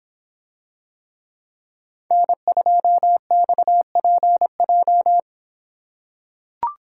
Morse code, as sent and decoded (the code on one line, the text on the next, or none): -. ..--- -..- .--. .---
N2XPJ